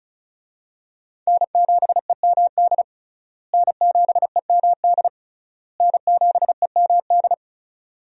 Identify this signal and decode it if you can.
Morse code, heard as a keyed tone, about 36 wpm, 700 Hz. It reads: N7EMD N7EMD N7EMD
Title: USA Call Signs - 5 Characters - Repeated 3x - 35wpm